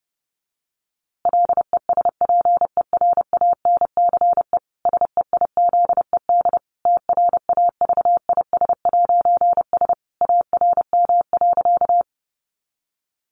Morse code, heard as a keyed tone, about 30 wpm, 700 Hz. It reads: LESPERANCE SEIZED TRAVIS'S ARM.
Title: A Sound Of Thunder 30wpm